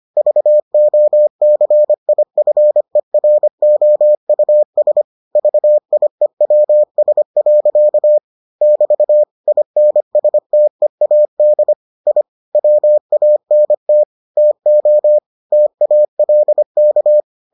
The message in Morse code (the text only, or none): VOCIFEROUS VIEWS. <BT> INSTEAD I WANT TO TALK